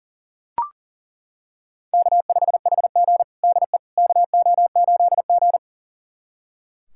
Morse code, heard as a keyed tone, about 40 wpm, 700 Hz. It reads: K5HG DE KO8G
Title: USA Callsign DE Callsign - 4 Characters - 40wpm